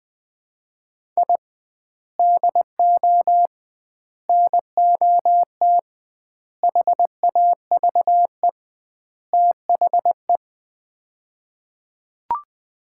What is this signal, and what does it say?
Morse code, heard as a keyed tone, about 20 wpm, 700 Hz. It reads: I DO NOT HAVE THE
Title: 2x Word-Spacing - Sets of 5 Words - 20wpm